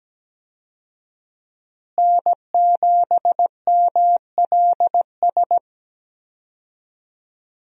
Morse code, heard as a keyed tone, about 17 wpm, 700 Hz. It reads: N7MLS
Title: USA Call Signs - 5 Characters - Repeated 2x - 17wpm